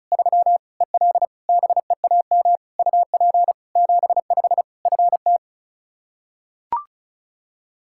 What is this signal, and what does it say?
Morse code, heard as a keyed tone, about 36 wpm, 700 Hz. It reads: 3 EL BEAM UP 75 FT
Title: K1USN SST QSO Practice 35wpm